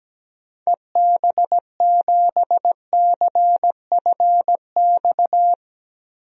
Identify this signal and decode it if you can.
Morse code, heard as a keyed tone, about 17 wpm, 700 Hz. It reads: EB7CFX
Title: DXCC-100 Call Signs - 6 Characters - 17wpm